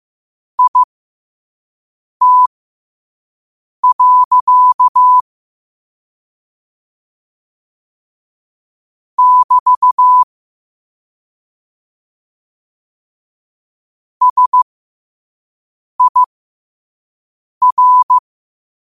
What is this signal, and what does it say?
Morse code, heard as a keyed tone, about 15 wpm, spaced as for 5 wpm, 1000 Hz. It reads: IT. <BT> SIR